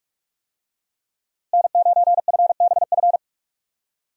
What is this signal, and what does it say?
Morse code, heard as a keyed tone, about 46 wpm, 700 Hz. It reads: N9FBF
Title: USA Call Signs - 5 Characters - Repeated 2x - 45wpm